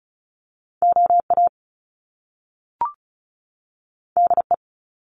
Morse code, heard as a keyed tone, about 35 wpm, 700 Hz. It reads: O A D E
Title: Top Two Letter Combinations in Words 35wpm